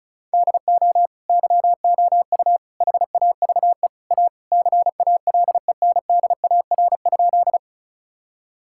Morse code, heard as a keyed tone, about 34 wpm, 700 Hz. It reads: DO YOU HAVE A CALENDAR?
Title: Sentences from Top 1600 Words 35wpm